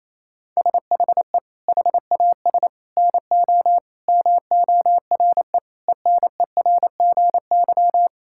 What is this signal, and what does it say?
Morse code, heard as a keyed tone, about 27 wpm, 700 Hz. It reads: SHE HAS NO MORE ENERGY
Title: Sentences from Top 1000 Words - Review Part 6 - 28wpm